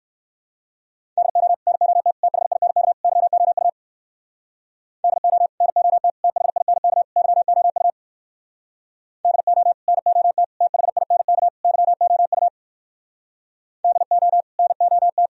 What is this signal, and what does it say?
Morse code, heard as a keyed tone, about 59 wpm, 700 Hz. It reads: DO NOT THINK YOU DO NOT THINK YOU DO NOT THINK YOU DO NOT